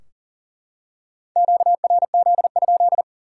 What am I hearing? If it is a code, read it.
Morse code, heard as a keyed tone, about 40 wpm, 700 Hz. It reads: QRZ?
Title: Top 90 QSO Elements 40wpm